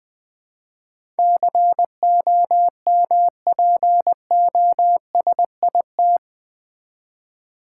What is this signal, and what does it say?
Morse code, heard as a keyed tone, about 20 wpm, 700 Hz. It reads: COMPOSIT
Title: Build A Word - 4 Letters At A Time - 11-Letter Words - 20wpm